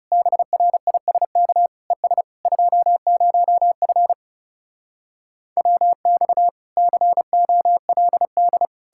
Morse code, heard as a keyed tone, about 40 wpm, 700 Hz. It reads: BRISK ES 20F WX COLD